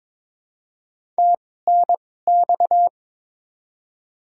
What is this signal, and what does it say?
Morse code, heard as a keyed tone, about 22 wpm, spaced as for 16 wpm, 700 Hz. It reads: TNX
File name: Top 20 QSO Elements - 2x Character Spacing - 22wpm